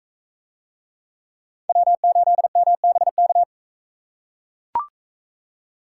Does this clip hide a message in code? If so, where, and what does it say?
Yes: Morse code from 1.7 to 3.4 s, heard as a keyed tone, about 42 wpm, 700 Hz. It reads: W8MBK